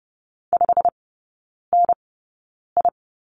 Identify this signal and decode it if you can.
Morse code, heard as a keyed tone, about 30 wpm, spaced as for 9 wpm, 700 Hz. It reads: 5NI
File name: Lesson 17e - Letter u - Three Characters 30wpm